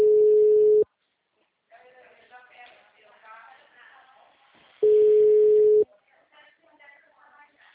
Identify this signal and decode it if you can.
Ringback tone (Europe, 425 Hz), with speech.
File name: out- -2016- - - .49582